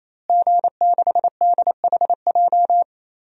Morse code, heard as a keyed tone, about 28 wpm, 700 Hz. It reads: G6DHJ
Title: DXCC-100 Call Signs - 5 Characters - Repeated 3x - 28wpm